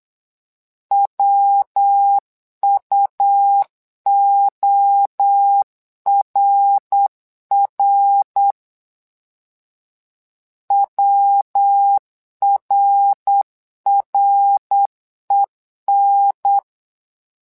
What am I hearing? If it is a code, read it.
Morse code, heard as a keyed tone, about 8 wpm, 800 Hz. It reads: WUORR WRREN
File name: LEZIONI / 3 MWRO / 3 MWRO 8wpm 2606